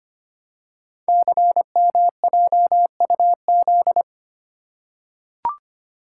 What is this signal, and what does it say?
Morse code, heard as a keyed tone, about 25 wpm, 700 Hz. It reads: CMJUZ
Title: Code Groups - 5 Letters - 25wpm